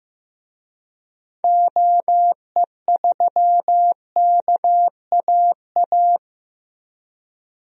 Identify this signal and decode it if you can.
Morse code, heard as a keyed tone, about 15 wpm, 700 Hz. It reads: OE3KAA
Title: DXCC-100 Call Signs - 6 Characters - Repeated 2x - 15wpm